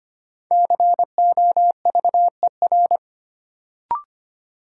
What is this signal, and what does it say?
Morse code, heard as a keyed tone, about 25 wpm, 700 Hz. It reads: COVER
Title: Sentences from Top 500 Words - New Word Drill - 25wpm